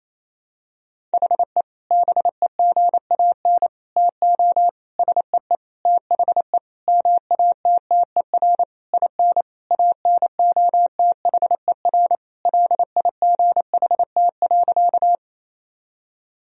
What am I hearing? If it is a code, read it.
Morse code, heard as a keyed tone, about 28 wpm, 700 Hz. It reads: HE BEGAN TO SEE THE MATTER IN ANOTHER LIGHT.
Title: Sentences from Top 1000 Words - Review Part 1 - 28wpm